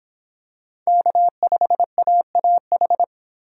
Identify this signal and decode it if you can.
Morse code, heard as a keyed tone, about 25 wpm, 700 Hz. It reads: K5AAH